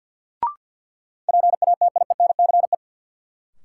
Morse code, heard as a keyed tone, about 47 wpm, 700 Hz. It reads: PATIENCE
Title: Sentences from Top 1300 Words - New Word Drill Alt - 50wpm